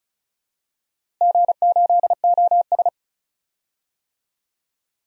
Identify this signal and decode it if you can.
Morse code, heard as a keyed tone, about 36 wpm, 700 Hz. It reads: G8OS
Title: DXCC-100 Call Signs - 4 Characters - Repeated 2x - 35wpm